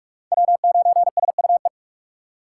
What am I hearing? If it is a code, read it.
Morse code, heard as a keyed tone, about 56 wpm, 700 Hz. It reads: W9SUE